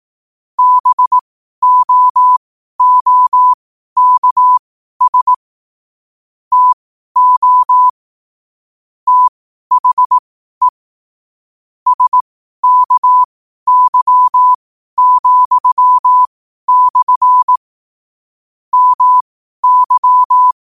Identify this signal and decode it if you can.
Morse code, heard as a keyed tone, about 18 wpm, spaced as for 12 wpm, 1000 Hz. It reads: BOOKS TO THE SKY,/ MY
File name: qotdfiles / 12wpm